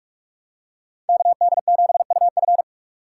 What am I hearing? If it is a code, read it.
Morse code, heard as a keyed tone, about 46 wpm, 700 Hz. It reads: KD7UF
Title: USA Call Signs - 5 Characters - 2x Character Spacing - 45wpm